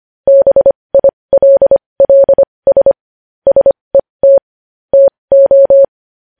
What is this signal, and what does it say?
Morse code, heard as a keyed tone, about 25 wpm, spaced as for 20 wpm, 550 Hz. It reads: BILLS SET TO
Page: news-headlines-in-morse-code-at-20-wpm